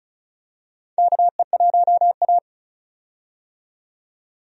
Morse code, heard as a keyed tone, about 36 wpm, 700 Hz. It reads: KE1A
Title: USA Call Signs - 4 Characters - 35wpm